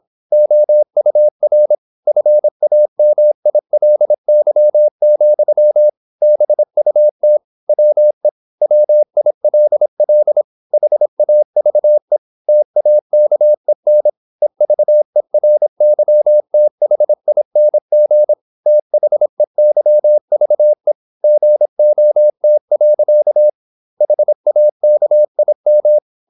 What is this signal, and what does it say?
Morse code, heard as a keyed tone, about 26 wpm, 600 Hz. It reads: OUR FAMILY, BUT WE WILL HAVE TAKEN EVERYTHING THEYVE GOT. HAKIM
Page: Never 26wpm | CW med Gnister